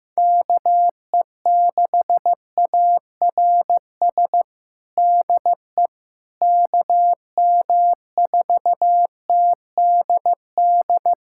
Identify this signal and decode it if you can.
Morse code, heard as a keyed tone, about 15 wpm, 700 Hz. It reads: KE6ARS DE KM4TDD
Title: USA Callsign DE Callsign - 6 Characters - 15wpm